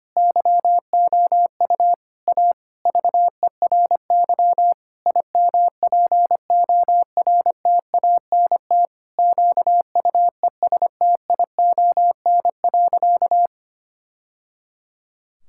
Morse code, heard as a keyed tone, about 25 wpm, 700 Hz. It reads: YOU A VERY IMPORTANT QUESTION.
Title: Sentences from Top 1000 Words - Review Part 6 - 25wpm